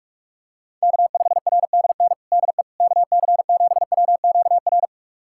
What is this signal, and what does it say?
Morse code, heard as a keyed tone, about 46 wpm, 700 Hz. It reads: K5RDN DE KC7WQR
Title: USA Callsign DE Callsign 45wpm